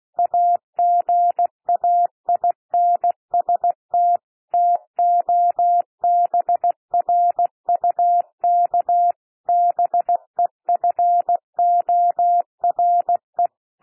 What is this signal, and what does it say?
Morse code, heard as a keyed tone, about 16 wpm, 700 Hz. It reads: AGAINST TOBRUK BEFORE
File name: cwmp3 / P1 SOUND PASSAGE 16 wpm